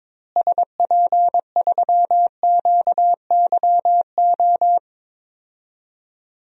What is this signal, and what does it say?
Morse code, heard as a keyed tone, about 22 wpm, 700 Hz. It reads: SP3QYO
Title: DXCC-100 Call Signs - 6 Characters - Repeated 3x - 22wpm